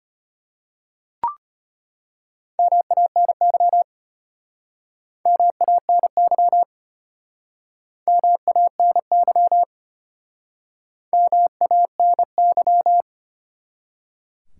Morse code, heard as a keyed tone, about 34 wpm, 700 Hz. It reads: MANY MANY MANY MANY